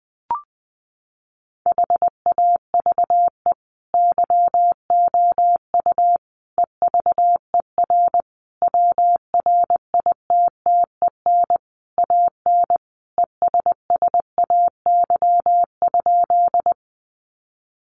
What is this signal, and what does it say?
Morse code, heard as a keyed tone, about 20 wpm, 700 Hz. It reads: HAVE YOU EVER WRITTEN AN ESSAY?
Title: Sentences from Top 1500 Words 20wpm